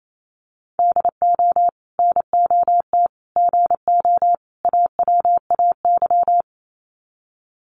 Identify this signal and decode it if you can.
Morse code, heard as a keyed tone, about 28 wpm, 700 Hz. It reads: DO NOT GO AWAY
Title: Sets of 4 Words 28wpm